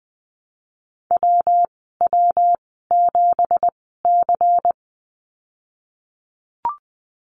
Morse code, heard as a keyed tone, about 20 wpm, spaced as for 15 wpm, 700 Hz. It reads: WW7C
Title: 207 Lesson 33g - Letter j - Call Signs 20wpm